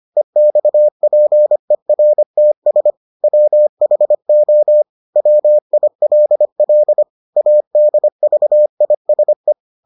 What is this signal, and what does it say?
Morse code, heard as a keyed tone, about 25 wpm, 600 Hz. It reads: EXPERTS WHO WILL ADVISE